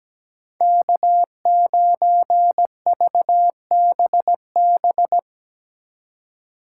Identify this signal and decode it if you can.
Morse code, heard as a keyed tone, about 19 wpm, 700 Hz. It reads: K9VBB
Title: Speed Racing - USA 1x3 Call Signs 17wpm